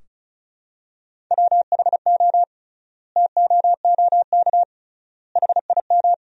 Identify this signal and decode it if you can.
Morse code, heard as a keyed tone, about 34 wpm, 700 Hz. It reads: WHO TOOK HIM
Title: 3x Word-Spacing - Sentences from Top 300 Words - 35wpm